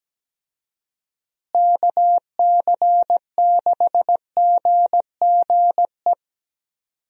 Morse code, heard as a keyed tone, about 17 wpm, 700 Hz. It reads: KC6GGE
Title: USA Call Signs - 6 Characters - Repeated 3x - 17wpm